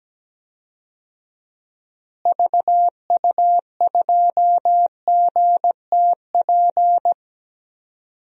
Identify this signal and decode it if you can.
Morse code, heard as a keyed tone, about 17 wpm, 700 Hz. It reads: VU2GTP
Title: DXCC-100 Call Signs - 6 Characters - Repeated 3x - 17wpm